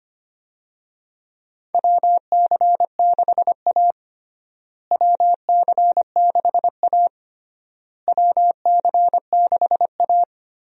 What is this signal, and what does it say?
Morse code, heard as a keyed tone, about 25 wpm, 700 Hz. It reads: WC6A WC6A WC6A
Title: USA Call Signs - 4 Characters - Repeated 3x - 25wpm